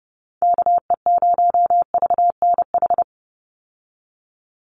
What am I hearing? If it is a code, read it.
Morse code, heard as a keyed tone, about 30 wpm, 700 Hz. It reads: KE0VNH
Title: USA Call Signs 30wpm